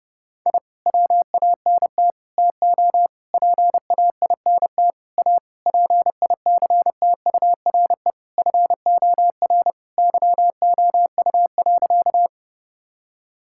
Morse code, heard as a keyed tone, about 30 wpm, 700 Hz. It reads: I WANT TO PAINT A PICTURE FOR YOU.
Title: Sentences from Top 1000 Words - Review Part 6 - 30wpm